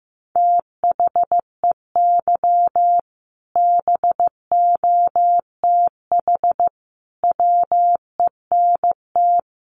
Morse code, heard as a keyed tone, about 15 wpm, 700 Hz. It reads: THEY BOTH WENT
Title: Sentences from Top 600 Words 15wpm